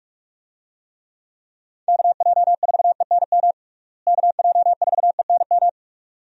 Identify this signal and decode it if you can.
Morse code, heard as a keyed tone, about 43 wpm, 700 Hz. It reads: KJ4ENM KJ4ENM
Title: USA Call Signs - 6 Characters - Repeated 2x - 45wpm